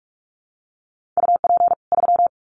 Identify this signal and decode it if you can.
Morse code, heard as a keyed tone, about 44 wpm, 700 Hz. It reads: UP 3
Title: Common QSO Phrases 45wpm